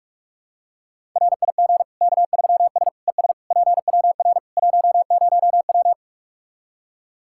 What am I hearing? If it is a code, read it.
Morse code, heard as a keyed tone, about 45 wpm, 700 Hz. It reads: RIG K3S ES PWR 10W
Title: Practice QSO - Exchange 2 - Info - 45wpm